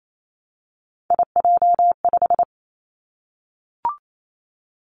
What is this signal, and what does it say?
Morse code, heard as a keyed tone, about 28 wpm, 700 Hz. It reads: IJ5
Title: Three Challenging Letter-Number Combinations 28wpm